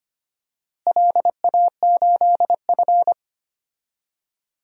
Morse code, heard as a keyed tone, about 25 wpm, 700 Hz. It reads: LA8F
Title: DXCC-100 Call Signs - 4 Characters - Repeated 2x - 25wpm